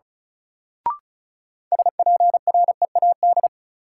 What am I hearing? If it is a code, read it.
Morse code, heard as a keyed tone, about 36 wpm, 700 Hz. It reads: SPREAD
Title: Sentences from Top 1200 Words - New Word Drill Alt - 35wpm